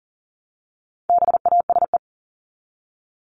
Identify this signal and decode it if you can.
Morse code, heard as a keyed tone, about 40 wpm, 700 Hz. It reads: BASE
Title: ED Suffix - Top 100 Words with Base Words 40wpm